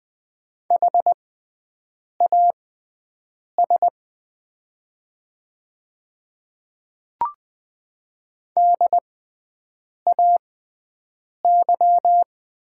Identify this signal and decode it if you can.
Morse code, heard as a keyed tone, about 20 wpm, spaced as for 7 wpm, 700 Hz. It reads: HAS DAY
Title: 6x Character Spacing - Top 100 Words - 20wpm